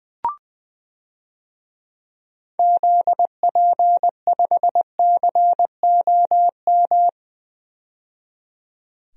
Morse code, heard as a keyed tone, about 20 wpm, 700 Hz. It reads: ZP5COM
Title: DXCC-100 Call Signs - 6 Characters - 20wpm